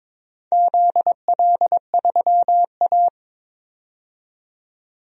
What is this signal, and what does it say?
Morse code, heard as a keyed tone, about 22 wpm, 700 Hz. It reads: ZL3A
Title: DXCC-100 Call Signs - 4 Characters - 22wpm